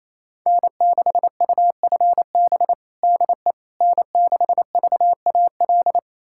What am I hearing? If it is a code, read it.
Morse code, heard as a keyed tone, about 28 wpm, 700 Hz. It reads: N6UFB DE N6VAL
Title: USA Callsign DE Callsign - 5 Characters - 28wpm